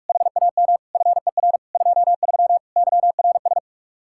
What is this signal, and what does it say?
Morse code, heard as a keyed tone, about 50 wpm, 700 Hz. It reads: HAM FER 23 YRS